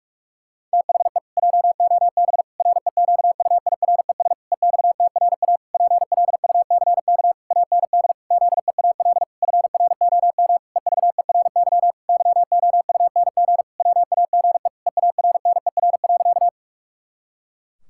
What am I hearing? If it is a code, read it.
Morse code, heard as a keyed tone, about 41 wpm, 700 Hz. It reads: THE JOB REQUIRES EXTRA PLUCK AND ZEAL FROM EVERY YOUNG WAGE EARNER.
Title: Pangram Challenge v2 - 45wpm